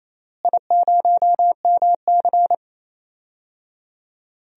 Morse code, heard as a keyed tone, about 28 wpm, 700 Hz. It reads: I0MC
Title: DXCC-100 Call Signs - 4 Characters - Repeated 3x - 28wpm